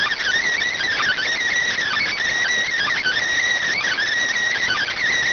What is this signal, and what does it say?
Emergency Alert System (SAME) tones (header bursts), heard as harsh buzzing bursts. It is heard from 0.7 to 5.1 s.